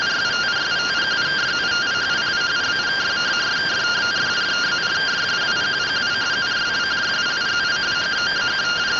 Two-tone teleprinter signal, about 170 Hz shift.